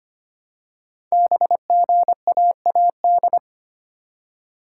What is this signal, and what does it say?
Morse code, heard as a keyed tone, about 25 wpm, 700 Hz. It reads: BGAAD
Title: Code Groups - 5 Letters - 25wpm